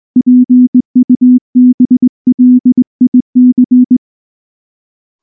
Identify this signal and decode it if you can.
Morse code, heard as a keyed tone, about 20 wpm, 250 Hz. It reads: PUBLIC
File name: morse / morse-training-20WPM-4